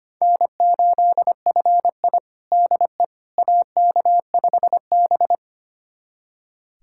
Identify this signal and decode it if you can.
Morse code, heard as a keyed tone, about 25 wpm, 700 Hz. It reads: N8FI DE AK5B
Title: USA Callsign DE Callsign - 4 Characters - 25wpm